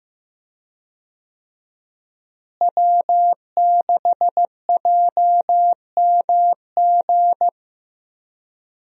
Morse code, heard as a keyed tone, about 15 wpm, 700 Hz. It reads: W6JMG
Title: USA Call Signs - 5 Characters - Repeated 2x - 15wpm